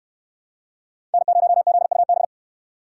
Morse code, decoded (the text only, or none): N0ZUB